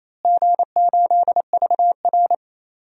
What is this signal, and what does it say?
Morse code, heard as a keyed tone, about 28 wpm, 700 Hz. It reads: G8VR
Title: DXCC-100 Call Signs - 4 Characters - Repeated 3x - 28wpm